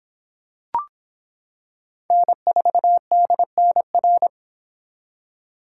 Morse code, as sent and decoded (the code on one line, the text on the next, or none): -. ....- -.. -. .-.
N4DNR